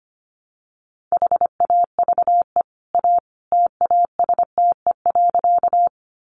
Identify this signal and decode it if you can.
Morse code, heard as a keyed tone, about 25 wpm, 700 Hz. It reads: HAVE A TASTE.
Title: Sentences from Top 1100 Words 25wpm